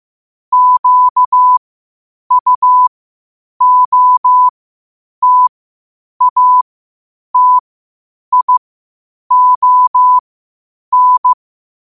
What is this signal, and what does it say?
Morse code, heard as a keyed tone, about 15 wpm, spaced as for 8 wpm, 1000 Hz. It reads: QUOTATION
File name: qotdfiles / 8wpm